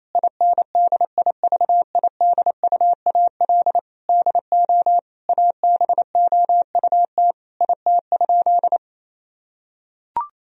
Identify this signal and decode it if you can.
Morse code, heard as a keyed tone, about 28 wpm, 700 Hz. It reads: INDIVIDUAL DO ABOUT IT?
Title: Sentences from Top 1000 Words - Review Part 4 - 28wpm